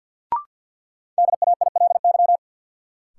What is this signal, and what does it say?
Morse code, heard as a keyed tone, about 52 wpm, 700 Hz. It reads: DAILY
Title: Sentences from Top 1100 Words - New Word Drill - 50wpm